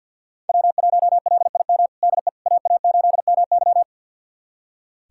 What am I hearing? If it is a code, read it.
Morse code, heard as a keyed tone, about 46 wpm, 700 Hz. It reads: W1LIM DE AA8MY
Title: USA Callsign DE Callsign - 5 Characters - 50wpm